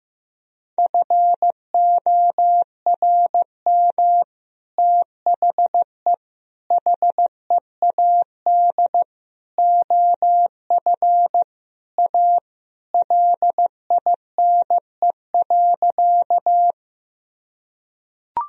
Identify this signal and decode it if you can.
Morse code, heard as a keyed tone, about 15 wpm, 700 Hz. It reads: FORM THE HEAD OF A LINE.
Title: Sentences from Top 1000 Words - Review Part 1 - 15wpm